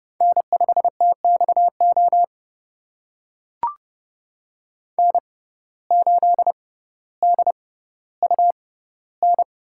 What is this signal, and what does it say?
Morse code, heard as a keyed tone, about 30 wpm, spaced as for 15 wpm, 700 Hz. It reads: N5TXO N8DUN
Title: USA Call Signs - 5 Characters - 6x Character Spacing - 30wpm